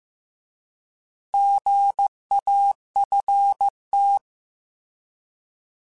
Morse code, heard as a keyed tone, about 15 wpm, 800 Hz. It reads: GAFT